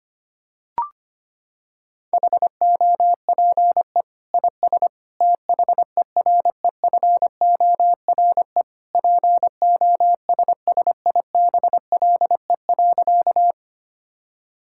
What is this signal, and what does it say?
Morse code, heard as a keyed tone, about 25 wpm, 700 Hz. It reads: HOPE IS THEREFORE POSSIBLE.
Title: Sentences from Top 1000 Words - Review Part 5 - 25wpm